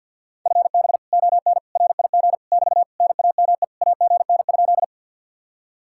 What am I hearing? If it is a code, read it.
Morse code, heard as a keyed tone, about 61 wpm, 700 Hz. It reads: FB ON RIG <BT> NAME AGN?
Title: Speed-Racing Common QSO Phrases - 50wpm